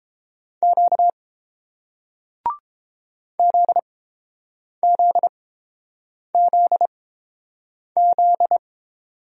Morse code, heard as a keyed tone, about 30 wpm, spaced as for 7 wpm, 700 Hz. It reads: Q ZZZZ